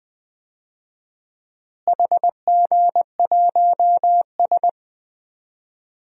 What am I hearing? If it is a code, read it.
Morse code, heard as a keyed tone, about 20 wpm, 700 Hz. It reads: HG1S
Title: DXCC-100 Call Signs - 4 Characters - Repeated 3x - 20wpm